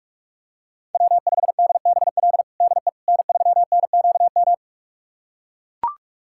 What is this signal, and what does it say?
Morse code, heard as a keyed tone, about 43 wpm, 700 Hz. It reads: W5DBL DE N3NQM
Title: USA Callsign DE Callsign - 5 Characters - 45wpm